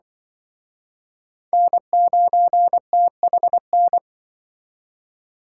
Morse code, heard as a keyed tone, about 23 wpm, 700 Hz. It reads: N9THN
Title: Speed Racing - USA 1x3 Call Signs 20wpm